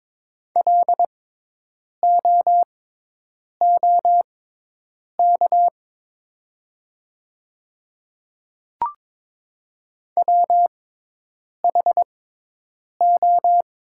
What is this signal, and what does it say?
Morse code, heard as a keyed tone, about 23 wpm, spaced as for 8 wpm, 700 Hz. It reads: LOOK WHO
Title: 6x Character Spacing - Top 100 Words - 22wpm